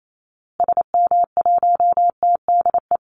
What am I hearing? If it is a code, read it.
Morse code, heard as a keyed tone, about 28 wpm, 700 Hz. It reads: SM1TDE
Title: Call Signs Repeated 2x - 28wpm